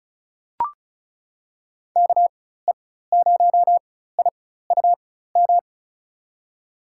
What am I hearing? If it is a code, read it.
Morse code, heard as a keyed tone, about 34 wpm, spaced as for 16 wpm, 700 Hz. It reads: KE0IUM
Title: USA Call Signs - 6 Characters - 4x Character Spacing - 35wpm